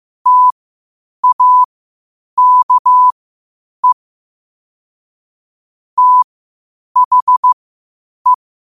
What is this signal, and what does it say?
Morse code, heard as a keyed tone, about 15 wpm, spaced as for 8 wpm, 1000 Hz. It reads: TAKE THE
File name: qotdfiles / 8wpm